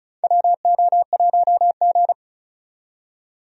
Morse code, heard as a keyed tone, about 36 wpm, 700 Hz. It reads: WO1G
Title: USA Call Signs - 4 Characters - 35wpm